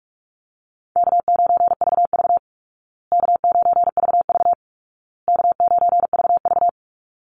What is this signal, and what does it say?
Morse code, heard as a keyed tone, about 44 wpm, 700 Hz. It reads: K9VV K9VV K9VV
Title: Call Signs Repeated 2x - 45wpm